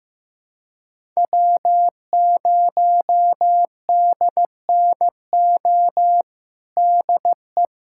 Morse code, heard as a keyed tone, about 15 wpm, 700 Hz. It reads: W0DNO DE
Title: USA Callsign DE Callsign - 5 Characters - 15wpm